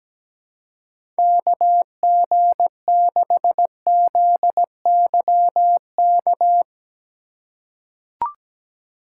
Morse code, heard as a keyed tone, about 17 wpm, 700 Hz. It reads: KG6ZYK
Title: USA Call Signs - 6 Characters - 2x Character Spacing - 17wpm